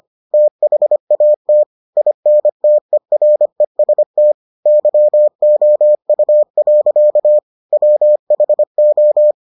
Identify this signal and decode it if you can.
Morse code, heard as a keyed tone, about 25 wpm, 600 Hz. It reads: THAT INTEREST YOU. WHO